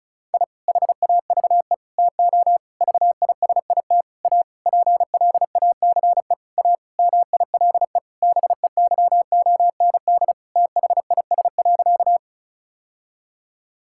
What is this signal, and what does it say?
Morse code, heard as a keyed tone, about 36 wpm, 700 Hz. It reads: I HAVE TO VISIT A PLACE A MILE BEYOND THIS.
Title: Sentences from Top 1000 Words - Review Part 6 - 35wpm